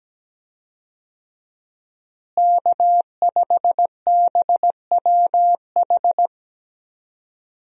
Morse code, heard as a keyed tone, about 17 wpm, 700 Hz. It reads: K5BWH
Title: USA Call Signs - 5 Characters - Repeated 2x - 17wpm